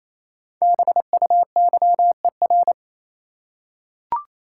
Morse code, heard as a keyed tone, about 28 wpm, 700 Hz. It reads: BUYER
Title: Five Letter Words - 28wpm